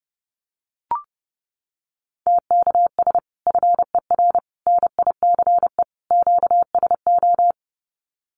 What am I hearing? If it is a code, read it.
Morse code, heard as a keyed tone, about 30 wpm, 700 Hz. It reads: TKS FER NICE QSO
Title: Common QSO Phrases 30wpm